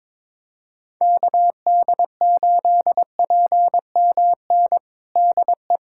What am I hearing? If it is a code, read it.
Morse code, heard as a keyed tone, about 22 wpm, 700 Hz. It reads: KD8PMN DE
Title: USA Callsign DE Callsign - 6 Characters - 22wpm